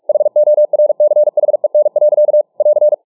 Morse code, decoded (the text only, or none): HORCHEN. <AR>